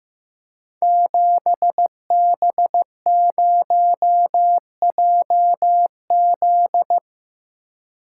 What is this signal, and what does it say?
Morse code, heard as a keyed tone, about 15 wpm, 700 Hz. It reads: 7B0JZ
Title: Code Groups - 5 Letter:Numbers - 15wpm